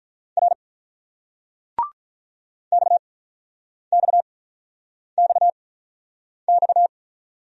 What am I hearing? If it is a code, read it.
Morse code, heard as a keyed tone, about 46 wpm, spaced as for 9 wpm, 700 Hz. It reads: R XXXX